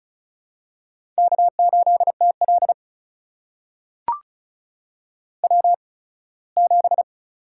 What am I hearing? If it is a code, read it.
Morse code, heard as a keyed tone, about 36 wpm, spaced as for 21 wpm, 700 Hz. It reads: K8TL W7
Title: USA Call Signs - 4 Characters - 8x Character Spacing - 35wpm